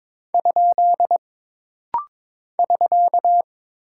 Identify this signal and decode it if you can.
Morse code, heard as a keyed tone, about 22 wpm, 700 Hz. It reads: ? <SK>
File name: practice / 22wpm / Single Punctuation-Prosign - Rapid-Fire - 22wpm